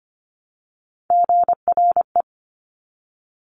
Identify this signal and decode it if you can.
Morse code, heard as a keyed tone, about 25 wpm, 700 Hz. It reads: GRE
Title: Top Three Letter Combinations in Words 25wpm